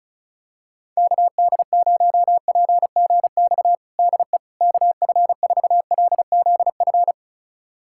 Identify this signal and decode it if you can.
Morse code, heard as a keyed tone, about 36 wpm, 700 Hz. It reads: KD0PGX DE KF4LZF
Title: USA Callsign DE Callsign 35wpm